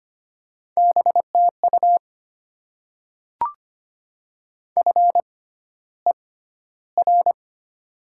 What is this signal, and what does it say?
Morse code, heard as a keyed tone, about 25 wpm, spaced as for 13 wpm, 700 Hz. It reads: BTU FER
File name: Top 20 QSO Elements - 6x Character Spacing - 25wpm